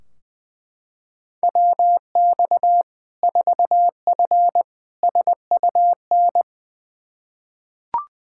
Morse code, heard as a keyed tone, about 21 wpm, 700 Hz. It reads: WX 4F SUN
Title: Practice QSO - Exchange 2 - Info - 20wpm